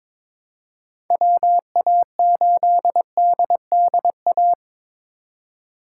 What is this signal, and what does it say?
Morse code, heard as a keyed tone, about 21 wpm, 700 Hz. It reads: WA8DDA
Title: USA Call Signs - 6 Characters - 2x Character Spacing - 22wpm